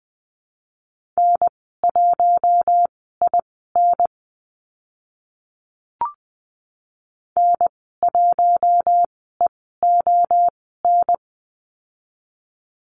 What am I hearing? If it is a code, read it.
Morse code, heard as a keyed tone, about 20 wpm, spaced as for 14 wpm, 700 Hz. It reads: N1IN N1EON
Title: 042 Lesson 8g - Number 1 - Call Signs 20wpm